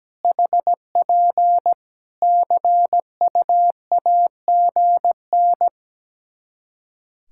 Morse code, heard as a keyed tone, about 17 wpm, 700 Hz. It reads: HP CUAGN
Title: Practice QSO - Exchange 4 - Ending - 17wpm